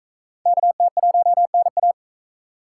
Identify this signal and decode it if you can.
Morse code, heard as a keyed tone, about 39 wpm, 700 Hz. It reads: KT1NA